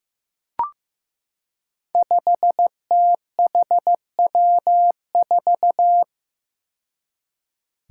Morse code, heard as a keyed tone, about 15 wpm, 700 Hz. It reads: 5THW4
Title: Five Challenging Letter-Number Combinations 15wpm